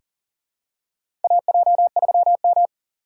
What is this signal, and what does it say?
Morse code, heard as a keyed tone, about 40 wpm, 690 Hz. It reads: AJ3M
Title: USA Call Signs - 8x Character Spacing - 40wpm